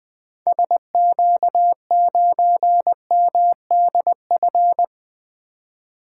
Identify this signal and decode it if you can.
Morse code, heard as a keyed tone, about 20 wpm, 700 Hz. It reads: SQ9MDF
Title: 2023 SOTA Activator Call Signs - 20wpm